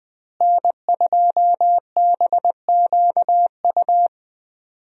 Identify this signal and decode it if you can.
Morse code, heard as a keyed tone, about 20 wpm, 700 Hz. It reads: N2BQU
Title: USA Call Signs - 5 Characters - Repeated 3x - 20wpm